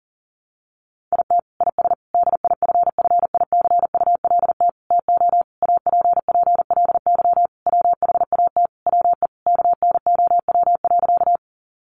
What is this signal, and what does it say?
Morse code, heard as a keyed tone, about 37 wpm, 700 Hz. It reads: IT IS DIFFICULT TO APPLY WHAT WE KNOW.
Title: Sentences from Top 1200 Words 40wpm